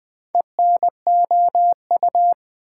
Morse code, heard as a keyed tone, about 20 wpm, 700 Hz. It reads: ENOU
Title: Build A Word - 2 Letters At A Time - 6-Letter Words - 20wpm